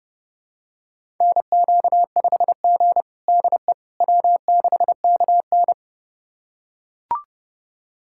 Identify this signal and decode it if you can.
Morse code, heard as a keyed tone, about 32 wpm, 700 Hz. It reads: NQ5G DE W6KN
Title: USA Callsign DE Callsign - 4 Characters - 30wpm